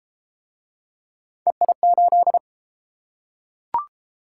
Morse code, heard as a keyed tone, about 33 wpm, 700 Hz. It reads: EI8